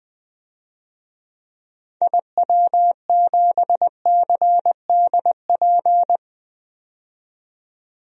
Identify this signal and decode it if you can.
Morse code, heard as a keyed tone, about 20 wpm, 700 Hz. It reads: IW7CDP
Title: DXCC-100 Call Signs - 6 Characters - 20wpm